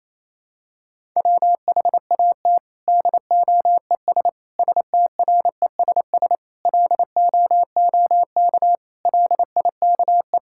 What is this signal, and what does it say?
Morse code, heard as a keyed tone, about 28 wpm, 700 Hz. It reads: WHAT DOES STRESS LOOK LIKE
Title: Sentences from Top 1000 Words - Review Part 6 - 28wpm